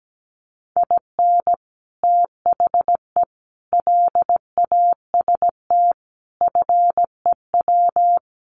Morse code, heard as a keyed tone, about 17 wpm, 700 Hz. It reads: IN THE LAST FEW
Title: Sets of 5 Words Encore2 17wpm